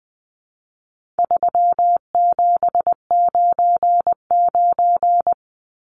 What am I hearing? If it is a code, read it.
Morse code, heard as a keyed tone, about 20 wpm, 700 Hz. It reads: 3799
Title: Numbers - 4-Digit 20wpm